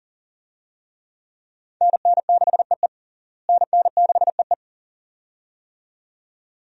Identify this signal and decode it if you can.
Morse code, heard as a keyed tone, about 40 wpm, 700 Hz. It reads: NN6EE NN6EE
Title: USA Call Signs - 5 Characters - Repeated 2x - 40wpm